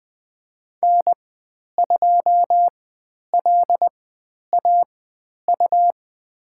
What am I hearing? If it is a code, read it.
Morse code, heard as a keyed tone, about 20 wpm, spaced as for 10 wpm, 700 Hz. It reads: N2LAU
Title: 109 Lesson 18g - Letter m - Call Signs 20wpm